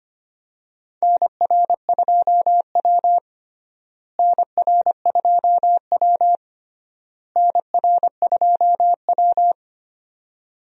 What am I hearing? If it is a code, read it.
Morse code, heard as a keyed tone, about 25 wpm, 700 Hz. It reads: NR2W NR2W NR2W
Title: USA Call Signs - 4 Characters - Repeated 3x - 25wpm